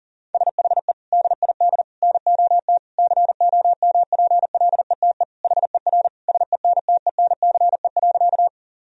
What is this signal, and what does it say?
Morse code, heard as a keyed tone, about 40 wpm, 700 Hz. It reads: SHE DID NOT COMPLETE HER SENTENCE.
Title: Sentences from Top 1000 Words - Review Part 2 - 40wpm